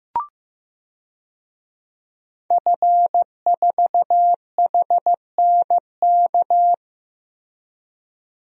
Morse code, heard as a keyed tone, about 15 wpm, 700 Hz. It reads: F4HNK
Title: DXCC-100 Call Signs - 5 Characters - Repeated 2x - 15wpm